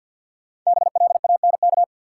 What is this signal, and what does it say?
Morse code, heard as a keyed tone, about 52 wpm, 700 Hz. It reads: BLANK
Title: Sentences from Top 1500 Words - New Word Drill - 50wpm